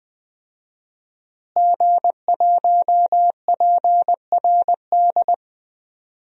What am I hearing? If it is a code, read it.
Morse code, heard as a keyed tone, about 20 wpm, 700 Hz. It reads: G1PRD
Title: DXCC-100 Call Signs - 5 Characters - Repeated 2x - 20wpm